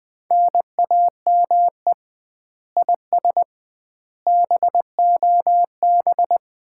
Morse code, heard as a keyed tone, about 20 wpm, 700 Hz. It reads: NAME IS BOB
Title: Common QSO Phrases - 2x Word-Spacing - 20wpm